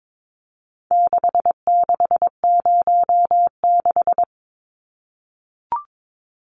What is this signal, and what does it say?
Morse code, heard as a keyed tone, about 22 wpm, 700 Hz. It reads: 6606
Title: Numbers - 4-Digit 22wpm